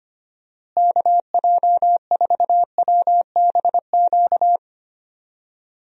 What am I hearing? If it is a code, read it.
Morse code, heard as a keyed tone, about 25 wpm, 700 Hz. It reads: KJ4WBQ
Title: USA Call Signs - 6 Characters - Repeated 2x - 25wpm